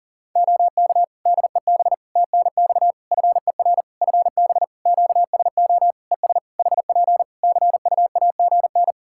Morse code, heard as a keyed tone, about 40 wpm, 700 Hz. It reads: OK DEB TNX FER FB QSO ES HP CUAGN
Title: Practice QSO - Exchange 4 - Ending - 40wpm